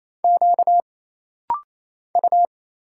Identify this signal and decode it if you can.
Morse code, heard as a keyed tone, about 27 wpm, 700 Hz. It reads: Q U
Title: Single Letter-Number Alt - Rapid-Fire 28wpm